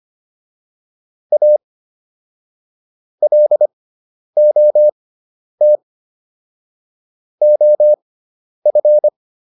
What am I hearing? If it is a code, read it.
Morse code, heard as a keyed tone, about 25 wpm, spaced as for 10 wpm, 600 Hz. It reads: A LOT OF